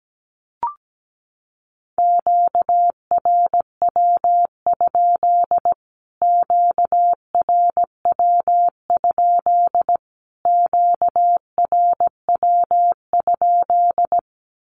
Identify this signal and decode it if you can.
Morse code, heard as a keyed tone, about 17 wpm, 700 Hz. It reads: QRW? QRW? QRW?
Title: Q-Codes Repeated 3x 17wpm